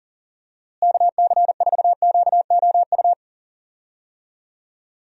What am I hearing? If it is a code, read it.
Morse code, heard as a keyed tone, about 37 wpm, 700 Hz. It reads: KC4QOU
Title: USA Call Signs - 6 Characters - Repeated 2x - 40wpm